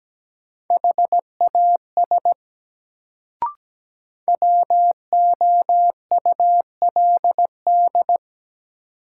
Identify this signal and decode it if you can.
Morse code, heard as a keyed tone, about 17 wpm, 700 Hz. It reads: HAS WOULD
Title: Sentences from Top 100 Words - New Word Drill - 17wpm